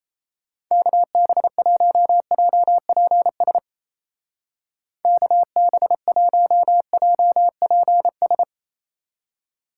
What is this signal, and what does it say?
Morse code, heard as a keyed tone, about 35 wpm, 700 Hz. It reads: KB1JPS KB1JPS
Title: Speed Racing - USA 2x3 Call Signs 28wpm